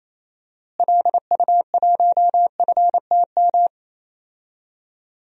Morse code, heard as a keyed tone, about 28 wpm, 700 Hz. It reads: LU1FTM
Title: DXCC-100 Call Signs - 6 Characters - Repeated 3x - 28wpm